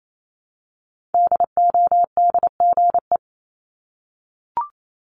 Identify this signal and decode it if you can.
Morse code, heard as a keyed tone, about 28 wpm, 700 Hz. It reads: DODGE
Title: Multi-letter Phonemes Repeated 3x 28wpm